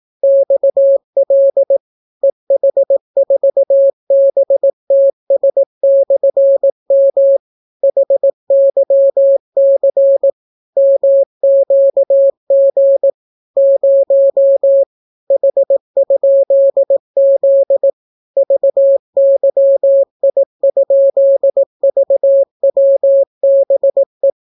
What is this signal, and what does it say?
Morse code, heard as a keyed tone, about 18 wpm, 550 Hz. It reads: XL EH4BTS/M HYC MQG 0 H?Z VYI?VWBE